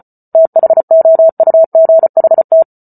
Morse code, heard as a keyed tone, about 34 wpm, 650 Hz. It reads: THOUGHT